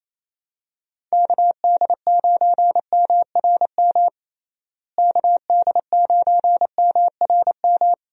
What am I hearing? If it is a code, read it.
Morse code, heard as a keyed tone, about 28 wpm, 700 Hz. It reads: KD9MRM KD9MRM
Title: USA Call Signs - 6 Characters - Repeated 2x - 28wpm